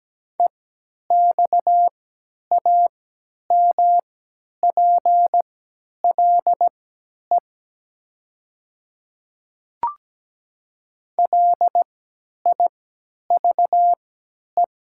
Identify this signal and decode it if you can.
Morse code, heard as a keyed tone, about 17 wpm, spaced as for 10 wpm, 700 Hz. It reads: EXAMPLE LIVE
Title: 3x Character Spacing - Top 300 Words - 17wpm